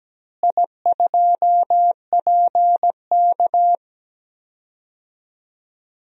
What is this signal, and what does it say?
Morse code, heard as a keyed tone, about 17 wpm, 700 Hz. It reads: I2PK
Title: DXCC-100 Call Signs - 4 Characters - Repeated 3x - 17wpm